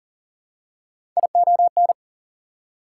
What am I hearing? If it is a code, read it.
Morse code, heard as a keyed tone, about 40 wpm, 700 Hz. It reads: ION
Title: Prefixes and Suffixes 40wpm